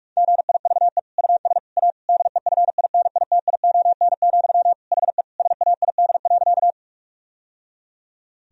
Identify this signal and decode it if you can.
Morse code, heard as a keyed tone, about 46 wpm, 700 Hz. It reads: GIVE US A DEFINITION, HE SAID.
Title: Sentences from Top 1000 Words - Review Part 4 - 45wpm